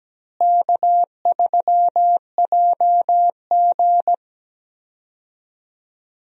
Morse code, heard as a keyed tone, about 17 wpm, 700 Hz. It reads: K3JG
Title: USA Call Signs - 4 Characters - Repeated 3x - 17wpm